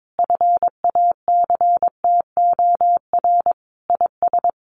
Morse code, heard as a keyed tone, about 22 wpm, 700 Hz. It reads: FACTOR IS
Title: Sentences from Top 1300 Words 22wpm